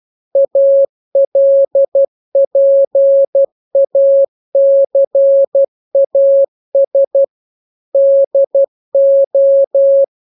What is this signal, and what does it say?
Morse code, heard as a keyed tone, about 12 wpm, 550 Hz. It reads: ALPACAS DO
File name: jokes / joke 14 12 0000